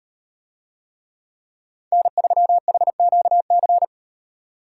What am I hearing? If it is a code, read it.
Morse code, heard as a keyed tone, about 35 wpm, 700 Hz. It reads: N3HQC